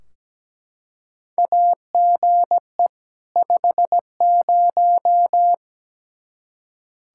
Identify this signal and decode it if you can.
Morse code, heard as a keyed tone, about 17 wpm, 700 Hz. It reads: AGE 50
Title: K1USN SST QSO Practice 17wpm